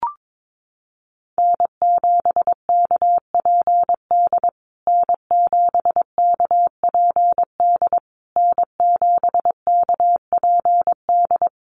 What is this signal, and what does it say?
Morse code, heard as a keyed tone, about 22 wpm, 700 Hz. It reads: N7KPD N7KPD N7KPD
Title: USA Call Signs Repeated 3x 22wpm